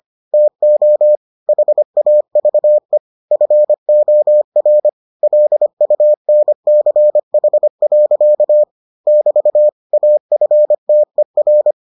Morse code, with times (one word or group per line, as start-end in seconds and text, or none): 0.3-1.2 TO
1.5-3.0 HAVE
3.3-4.9 FOR
5.2-8.6 LUNCH.
9.1-9.7 <BT>
9.9-11.7 AFTER